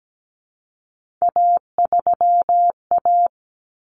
Morse code, heard as a keyed tone, about 17 wpm, 700 Hz. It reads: A3A
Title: Common Cut Numbers - 3-Digit 17wpm